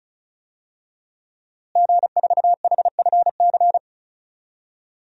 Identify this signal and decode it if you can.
Morse code, heard as a keyed tone, about 36 wpm, 700 Hz. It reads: G4HFC
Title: DXCC-100 Call Signs - 5 Characters - Repeated 3x - 35wpm